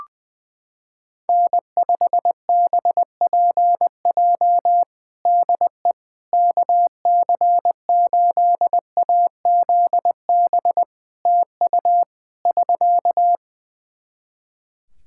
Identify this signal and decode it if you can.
Morse code, heard as a keyed tone, about 20 wpm, 700 Hz. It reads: N5BPJ DE KC8AZB TU <SK>
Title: K1USN SST QSO Practice 20wpm